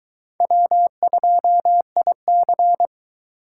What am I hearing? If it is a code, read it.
Morse code, heard as a keyed tone, about 22 wpm, 700 Hz. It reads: W2IC